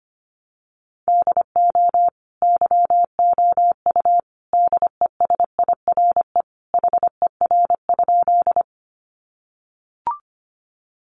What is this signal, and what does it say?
Morse code, heard as a keyed tone, about 25 wpm, 700 Hz. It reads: DO YOU DESIRE HER?
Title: Sentences from Top 1100 Words 25wpm